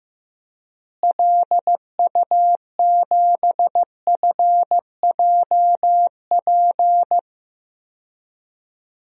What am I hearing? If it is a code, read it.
Morse code, heard as a keyed tone, about 15 wpm, 700 Hz. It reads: LU7FJP
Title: DXCC-100 Call Signs - 6 Characters - Repeated 3x - 15wpm